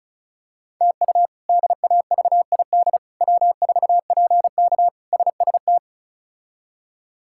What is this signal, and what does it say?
Morse code, heard as a keyed tone, about 34 wpm, 700 Hz. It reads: TU DAVID W4PK SST
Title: K1USN SST QSO Practice - 35wpm